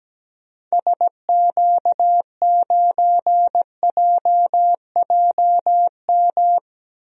Morse code, heard as a keyed tone, about 17 wpm, 700 Hz. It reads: SQ9JJM
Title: DXCC-100 Call Signs - 6 Characters - 17wpm